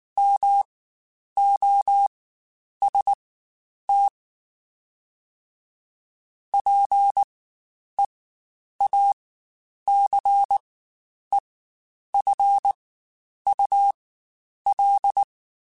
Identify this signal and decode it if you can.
Morse code, heard as a keyed tone, about 19 wpm, spaced as for 9 wpm, 800 Hz. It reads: MOST PEACEFUL
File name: Quotes Spaced / Quotes Spaced-mp3s / ChildAsleep sp @18